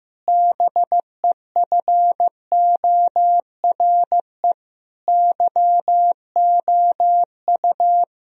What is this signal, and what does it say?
Morse code, heard as a keyed tone, about 15 wpm, 700 Hz. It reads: BEFORE YOU
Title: Sentences from Top 1000 Words - Review Part 3 - 15wpm